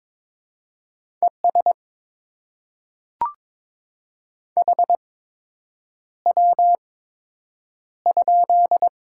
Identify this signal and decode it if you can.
Morse code, heard as a keyed tone, about 22 wpm, spaced as for 8 wpm, 700 Hz. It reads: ES HW?
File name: Top 20 QSO Elements - 8x Character Spacing - 22wpm